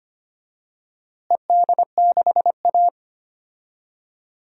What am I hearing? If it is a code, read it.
Morse code, heard as a keyed tone, about 25 wpm, 700 Hz. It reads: ED6A
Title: DXCC-100 Call Signs - 4 Characters - Repeated 2x - 25wpm